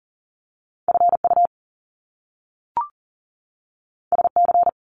Morse code, heard as a keyed tone, about 40 wpm, 700 Hz. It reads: FU SC
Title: Top Two Letter Combinations in Words 40wpm